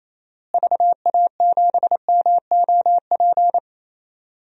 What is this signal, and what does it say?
Morse code, heard as a keyed tone, about 27 wpm, 700 Hz. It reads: VA7MOP
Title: DXCC-100 Call Signs - 6 Characters - Repeated 3x - 28wpm